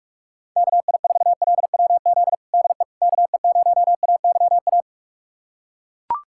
Morse code, heard as a keyed tone, about 41 wpm, 700 Hz. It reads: KI4LWZ DE KE0AYA
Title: USA Callsign DE Callsign - 6 Characters - 45wpm